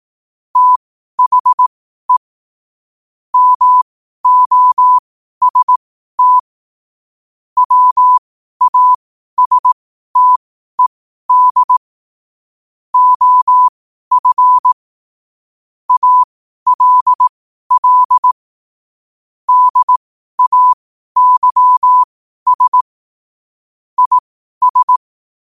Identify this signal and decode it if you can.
Morse code, heard as a keyed tone, about 18 wpm, spaced as for 12 wpm, 1000 Hz. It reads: THE MOST WASTED OF ALL DAYS IS
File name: qotdfiles / 12wpm